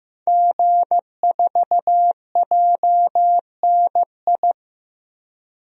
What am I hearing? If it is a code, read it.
Morse code, heard as a keyed tone, about 15 wpm, 700 Hz. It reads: G4JNI
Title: DXCC-100 Call Signs - 5 Characters - 15wpm